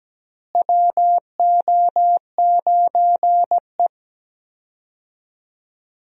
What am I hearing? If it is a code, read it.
Morse code, heard as a keyed tone, about 17 wpm, 700 Hz. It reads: WO9E
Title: USA Call Signs - 4 Characters - Repeated 2x - 17wpm